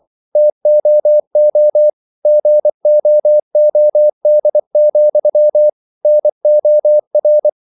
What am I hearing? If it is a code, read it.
Morse code, heard as a keyed tone, about 24 wpm, 600 Hz. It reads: TOO GOOD, NOR
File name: CW / Practice / If24